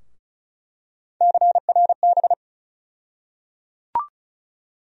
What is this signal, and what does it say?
Morse code, heard as a keyed tone, about 34 wpm, 700 Hz. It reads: CRB
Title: USA Call Sign Postfixes - 3 Letters 35wpm